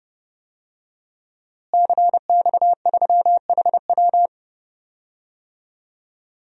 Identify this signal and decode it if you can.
Morse code, heard as a keyed tone, about 30 wpm, 700 Hz. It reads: CX3HW
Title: DXCC-100 Call Signs - 5 Characters - Repeated 3x - 30wpm